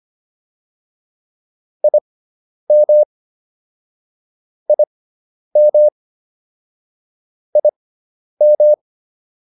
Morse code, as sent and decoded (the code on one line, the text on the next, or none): .. -- / .. -- / .. --
IM IM IM